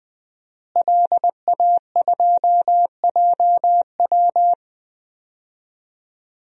Morse code, heard as a keyed tone, about 20 wpm, 700 Hz. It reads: LA2JW
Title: DXCC-100 Call Signs - 5 Characters - Repeated 2x - 20wpm